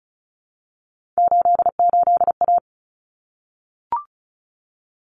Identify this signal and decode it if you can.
Morse code, heard as a keyed tone, about 34 wpm, 700 Hz. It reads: 88A
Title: Common Cut Numbers - 3-Digit 35wpm